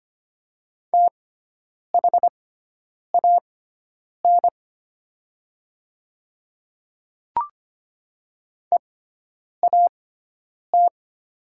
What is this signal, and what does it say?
Morse code, heard as a keyed tone, about 25 wpm, spaced as for 9 wpm, 700 Hz. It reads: THAN EAT
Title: 6x Character Spacing - Top 300 Words - 25wpm